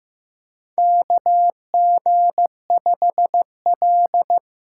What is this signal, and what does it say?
Morse code, heard as a keyed tone, about 15 wpm, 700 Hz. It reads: KG5L
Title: USA Call Signs - 6 Characters - 8x Character Spacing - 15wpm